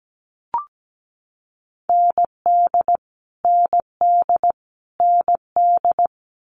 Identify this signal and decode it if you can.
Morse code, heard as a keyed tone, about 17 wpm, 700 Hz. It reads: ND ND ND
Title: Multi-letter Phonemes Repeated 3x 17wpm